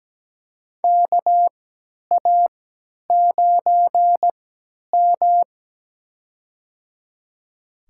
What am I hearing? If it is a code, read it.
Morse code, heard as a keyed tone, about 17 wpm, spaced as for 10 wpm, 700 Hz. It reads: KA9M
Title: USA Call Signs - 4 Characters - 3x Character Spacing - 17wpm